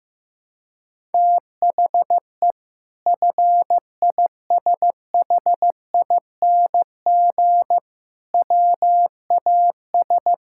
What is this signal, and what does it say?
Morse code, heard as a keyed tone, about 15 wpm, 700 Hz. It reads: THE FISHING WAS
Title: Sentences from Top 1000 Words - Review Part 5 - 15wpm